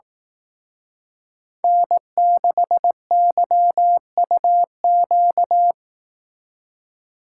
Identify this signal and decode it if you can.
Morse code, heard as a keyed tone, about 18 wpm, 700 Hz. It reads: N6YUQ
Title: Speed Racing - USA 1x3 Call Signs 15wpm